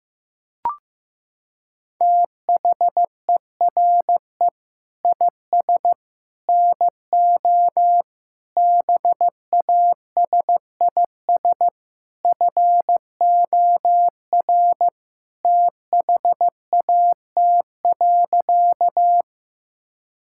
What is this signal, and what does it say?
Morse code, heard as a keyed tone, about 15 wpm, 700 Hz. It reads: THERE IS NO BASIS FOR THAT.
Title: Sentences from Top 1000 Words - Review Part 5 - 15wpm